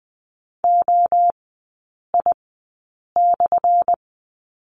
Morse code, heard as a keyed tone, about 20 wpm, spaced as for 8 wpm, 700 Hz. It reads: OI/
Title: 178 Lesson 29e - Slash - Three Characters 20wpm